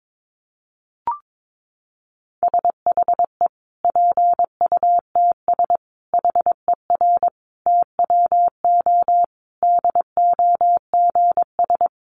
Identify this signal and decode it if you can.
Morse code, heard as a keyed tone, about 22 wpm, 700 Hz. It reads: SHE PUTS HER TWO DOGS
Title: Easy Phrases 22wpm